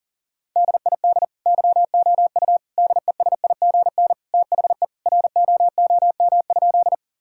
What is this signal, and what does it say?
Morse code, heard as a keyed tone, about 40 wpm, 700 Hz. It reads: DID YOU DESIGN THE ROOM?
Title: Sentences from Top 1000 Words - Review Part 5 - 40wpm